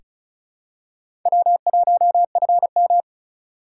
Morse code, heard as a keyed tone, about 34 wpm, 700 Hz. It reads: W1FM
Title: USA Call Signs - 4 Characters - 3x Character Spacing - 35wpm